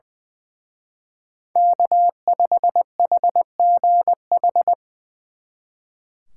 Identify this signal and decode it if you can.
Morse code, heard as a keyed tone, about 24 wpm, 700 Hz. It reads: K5HGH
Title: Speed Racing - USA Call Signs 20wpm